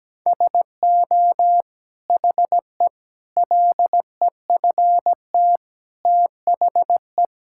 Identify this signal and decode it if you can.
Morse code, heard as a keyed tone, about 17 wpm, 700 Hz. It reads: SO HE LEFT THE
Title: Sentences from Top 1600 Words 17wpm